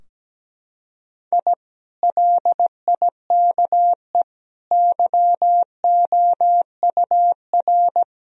Morse code, heard as a keyed tone, about 17 wpm, 700 Hz. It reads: I LIKE YOUR
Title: Sentences from Top 1000 Words - Review Part 4 - 17wpm